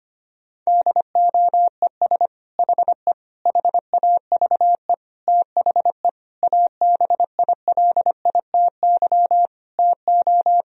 Morse code, heard as a keyed tone, about 25 wpm, 700 Hz. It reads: DOES HE HAVE THE ABILITY TO
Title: Sentences from Top 1000 Words - Review Part 3 - 25wpm